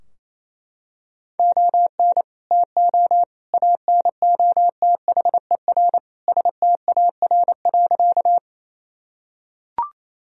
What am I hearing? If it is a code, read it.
Morse code, heard as a keyed tone, about 28 wpm, 700 Hz. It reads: ON TO ANOTHER STAR.
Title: Sentences from Top 1000 Words - Review Part 2 - 28wpm